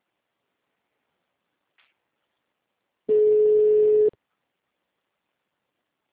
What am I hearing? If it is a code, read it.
Ringback tone (Europe, 425 Hz).